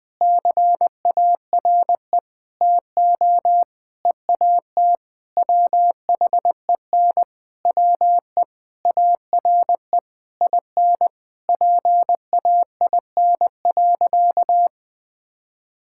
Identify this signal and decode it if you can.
Morse code, heard as a keyed tone, about 20 wpm, 700 Hz. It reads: CARE TO EAT WHEN WE ARE IN PAIN.
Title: Sentences from Top 1600 Words 20wpm